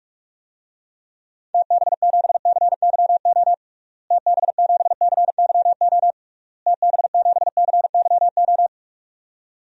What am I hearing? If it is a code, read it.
Morse code, heard as a keyed tone, about 46 wpm, 700 Hz. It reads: TB7CYO TB7CYO TB7CYO
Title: DXCC-100 Call Signs - 6 Characters - Repeated 3x - 45wpm